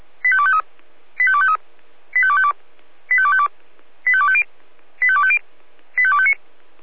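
Five-tone selective call (ZVEI), heard as quick runs of five tones, tones of about 70 ms. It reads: 85253 85253 85252 85252 85250 85250 85259
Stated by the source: ZVEI five-tone selective call received on SDR.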